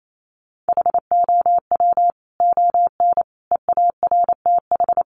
Morse code, heard as a keyed tone, about 28 wpm, 700 Hz. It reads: HOW ON EARTH
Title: Sentences from Top 200 Words 28wpm